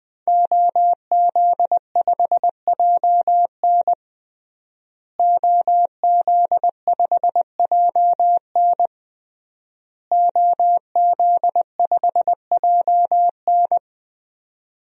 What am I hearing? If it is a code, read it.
Morse code, heard as a keyed tone, about 20 wpm, 700 Hz. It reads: OZ5JN OZ5JN OZ5JN
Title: DXCC-100 Call Signs - 5 Characters - Repeated 3x - 20wpm